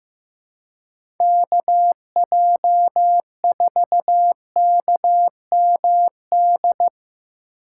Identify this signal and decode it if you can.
Morse code, heard as a keyed tone, about 15 wpm, 700 Hz. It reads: KJ4KMD
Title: USA Call Signs - 4x Character Spacing - 15wpm